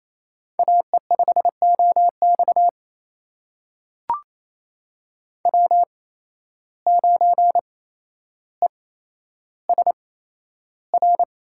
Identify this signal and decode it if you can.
Morse code, heard as a keyed tone, about 28 wpm, spaced as for 12 wpm, 700 Hz. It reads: AE5OX W9ESR
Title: USA Call Signs - 5 Characters - 8x Character Spacing - 28wpm